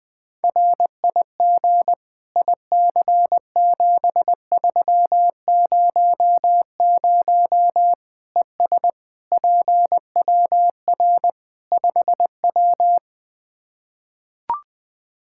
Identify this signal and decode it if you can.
Morse code, heard as a keyed tone, about 21 wpm, 700 Hz. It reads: RIG IC7300 ES PWR 5W
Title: Practice QSO - Exchange 2 - Info - 20wpm